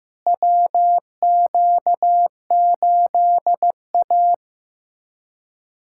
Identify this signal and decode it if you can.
Morse code, heard as a keyed tone, about 15 wpm, 700 Hz. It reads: WQ8A
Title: USA Callsign DE Callsign - 4 Characters - 15wpm